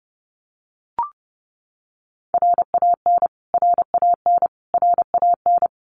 Morse code, heard as a keyed tone, about 28 wpm, 700 Hz. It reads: RAN RAN RAN
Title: Top 500 Words Repeated 30wpm